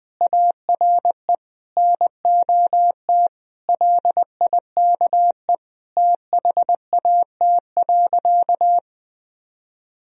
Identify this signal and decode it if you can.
Morse code, heard as a keyed tone, about 20 wpm, 700 Hz. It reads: ARE NOT LIKE THAT.
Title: Sentences from Top 1000 Words - Review Part 3 - 20wpm